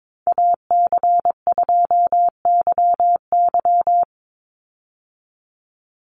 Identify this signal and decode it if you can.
Morse code, heard as a keyed tone, about 22 wpm, 700 Hz. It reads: AC2YY
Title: USA Call Signs 22wpm